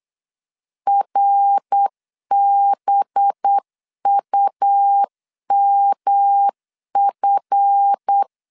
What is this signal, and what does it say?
Morse code, heard as a keyed tone, about 8 wpm, 800 Hz. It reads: RBUMF
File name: LEZIONI / 7 GF9Z / 7 GF 8wpm 3108